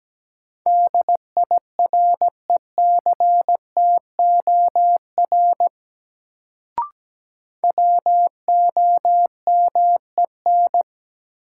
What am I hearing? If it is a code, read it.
Morse code, heard as a keyed tone, about 17 wpm, 700 Hz. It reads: DIRECTOR WOMEN
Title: Sentences from Top 1300 Words - New Word Drill Alt - 17wpm